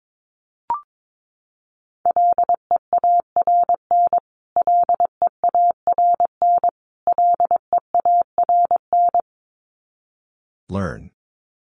Morse code, heard as a keyed tone, about 22 wpm, 700 Hz. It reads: LEARN LEARN LEARN
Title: Top 300 Words Repeated 22wpm